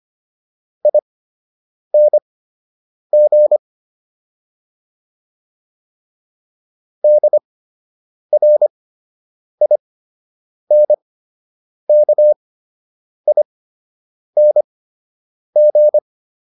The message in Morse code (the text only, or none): ING DRINKING